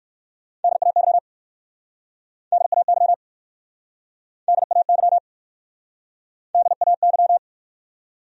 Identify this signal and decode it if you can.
Morse code, heard as a keyed tone, about 57 wpm, 700 Hz. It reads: DAY DAY DAY DAY